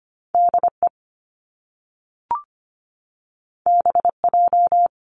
Morse code, heard as a keyed tone, about 25 wpm, 700 Hz. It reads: DE BJ
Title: Top Two Letter Combinations in Words 25wpm